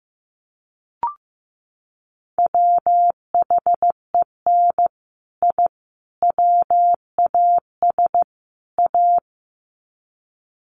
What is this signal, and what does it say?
Morse code, heard as a keyed tone, about 15 wpm, 700 Hz. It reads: WHEN I WAS A
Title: Sets of 4 Words 15wpm